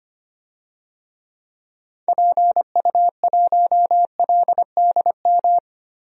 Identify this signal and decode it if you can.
Morse code, heard as a keyed tone, about 25 wpm, 700 Hz. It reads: PU1LDM
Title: DXCC-100 Call Signs - 6 Characters - Repeated 3x - 25wpm